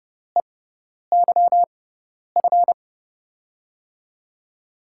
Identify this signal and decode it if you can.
Morse code, heard as a keyed tone, about 30 wpm, spaced as for 10 wpm, 700 Hz. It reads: EYF
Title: USA Call Signs - 6 Characters - 6x Character Spacing - 30wpm